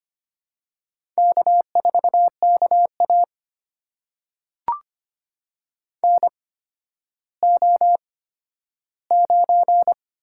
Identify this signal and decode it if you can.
Morse code, heard as a keyed tone, about 25 wpm, spaced as for 12 wpm, 700 Hz. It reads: K4KA NO9
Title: USA Call Signs - 4 Characters - 8x Character Spacing - 25wpm